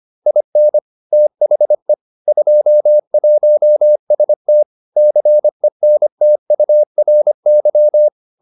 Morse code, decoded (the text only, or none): IN THE 21ST CENTURY